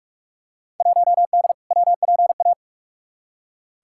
Morse code, heard as a keyed tone, about 46 wpm, 700 Hz. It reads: 1D WPA
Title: Field Day Exchanges - 45wpm